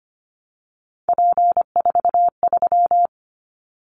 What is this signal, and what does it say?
Morse code, heard as a keyed tone, about 25 wpm, 700 Hz. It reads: P43
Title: Callsign Prefixes for Top 100 DXCC - 25wpm